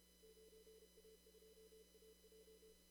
Morse code, heard as a keyed tone, about 29 wpm, 440 Hz, with noise. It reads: 9A2AJ
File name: CW / kt5j wpx cw-op-k5pi